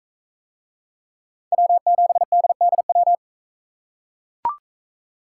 Morse code, heard as a keyed tone, about 42 wpm, 700 Hz. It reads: W7DDW